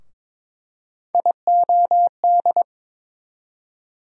Morse code, heard as a keyed tone, about 22 wpm, 700 Hz. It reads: IOD
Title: USA Call Sign Postfixes - 3 Letters 22wpm